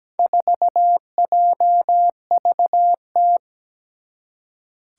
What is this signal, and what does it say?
Morse code, heard as a keyed tone, about 17 wpm, 700 Hz. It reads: 4JVT
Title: Four Challenging Letter-Number Combinations 17wpm